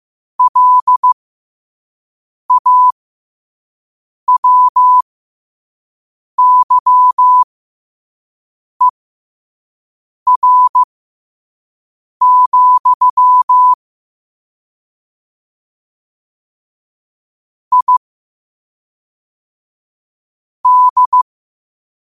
Morse code, heard as a keyed tone, about 15 wpm, spaced as for 5 wpm, 1000 Hz. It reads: LAWYER, I D